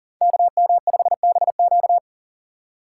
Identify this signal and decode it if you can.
Morse code, heard as a keyed tone, about 40 wpm, 700 Hz. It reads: KM5BQ
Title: USA Call Signs - 5 Characters - 8x Character Spacing - 40wpm